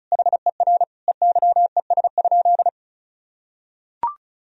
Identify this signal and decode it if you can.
Morse code, heard as a keyed tone, about 36 wpm, 700 Hz. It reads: HER EYES?
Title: Sentences from Top 1000 Words - Review Part 4 - 35wpm